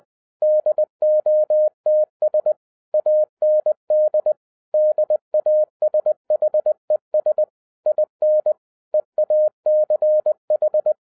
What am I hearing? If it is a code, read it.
Morse code, heard as a keyed tone, about 20 wpm, 610 Hz. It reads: DOTS AND DASHES IN EACH
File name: 2011-10-mp3 / n13-TIPS FROM MY 34 YEARS OF EXPERIENCE 20wpm